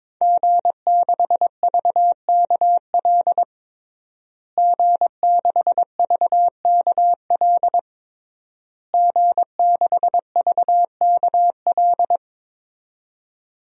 Morse code, heard as a keyed tone, about 22 wpm, 700 Hz. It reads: G6VKL G6VKL G6VKL
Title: DXCC-100 Call Signs - 5 Characters - Repeated 3x - 22wpm